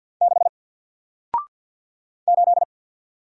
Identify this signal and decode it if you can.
Morse code, heard as a keyed tone, about 50 wpm, 700 Hz. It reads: 6 8
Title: Single Letter-Number Alt 50wpm